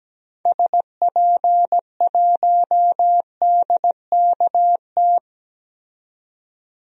Morse code, heard as a keyed tone, about 17 wpm, 700 Hz. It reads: SP1DKT
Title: DXCC-100 Call Signs - 6 Characters - 17wpm